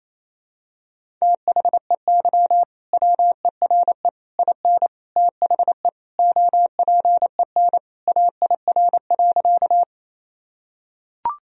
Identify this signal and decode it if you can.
Morse code, heard as a keyed tone, about 28 wpm, 700 Hz. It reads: THEY WERE IN THE OPEN AIR.
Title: Sentences from Top 1000 Words - Review Part 1 - 28wpm